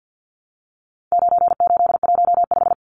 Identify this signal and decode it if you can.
Morse code, heard as a keyed tone, about 50 wpm, 700 Hz. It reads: 9815
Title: Numbers - 4-Digit 50wpm